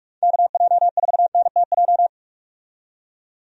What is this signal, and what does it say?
Morse code, heard as a keyed tone, about 46 wpm, 700 Hz. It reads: KJ4NTJ
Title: USA Call Signs - 6 Characters - 45wpm